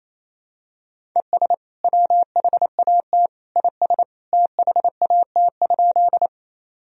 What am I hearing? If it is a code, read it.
Morse code, heard as a keyed tone, about 28 wpm, 700 Hz. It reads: ES WHAT IS THAT?
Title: Sentences from Top 100 Words and Abbreviations 28wpm